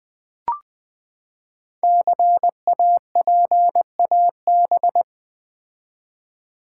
Morse code, heard as a keyed tone, about 20 wpm, 700 Hz. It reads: CAPAB
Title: Build A Word - 5 Letters At A Time - 10-Letter Words - 20wpm